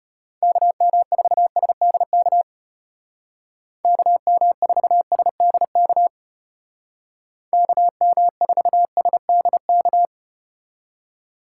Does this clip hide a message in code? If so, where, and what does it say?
Yes: Morse code from 0.4 to 10.1 s, heard as a keyed tone, about 34 wpm, 700 Hz. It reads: KM4SDK KM4SDK KM4SDK